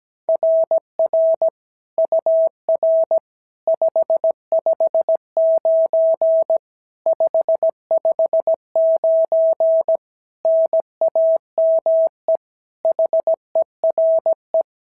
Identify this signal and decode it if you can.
Morse code, heard as a keyed tone, about 17 wpm, 660 Hz. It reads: RR UR 559 559 NAME HERE
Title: SOTA QSO Practice 17wpm